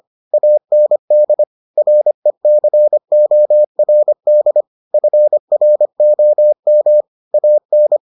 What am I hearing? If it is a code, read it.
Morse code, heard as a keyed tone, about 25 wpm, 600 Hz. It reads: AND RECORD FROM AN